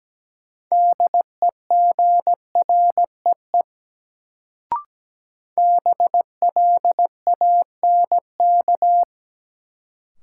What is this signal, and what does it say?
Morse code, heard as a keyed tone, about 17 wpm, 700 Hz. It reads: DEGREE BLANK
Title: Sentences from Top 1500 Words - New Word Drill - 17wpm